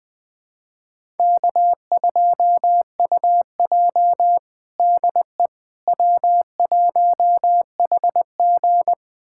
Morse code, heard as a keyed tone, about 19 wpm, 700 Hz. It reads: K2UJ DE W1HG
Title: USA Callsign DE Callsign - 4 Characters - 20wpm